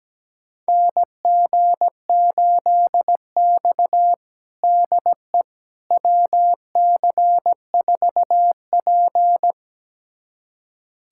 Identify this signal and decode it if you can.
Morse code, heard as a keyed tone, about 17 wpm, 700 Hz. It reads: NG8X DE WC4P
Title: USA Callsign DE Callsign - 4 Characters - 17wpm